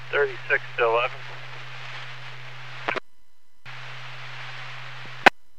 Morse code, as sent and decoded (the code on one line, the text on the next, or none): none